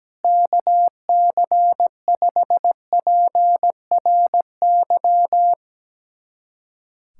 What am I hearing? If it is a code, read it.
Morse code, heard as a keyed tone, about 17 wpm, 700 Hz. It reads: KC5PRY
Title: USA Callsign DE Callsign - 6 Characters - 17wpm